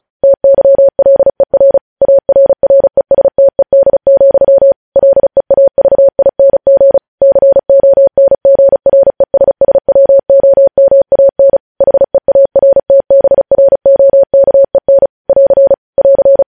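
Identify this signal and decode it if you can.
Morse code, heard as a keyed tone, about 36 wpm, 550 Hz. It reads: TYLER ARRESTED, LEAVING CONGRESSWOMAN HEARTBROKEN <AR> <AR>
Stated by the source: News headlines sent in Morse at speed 35.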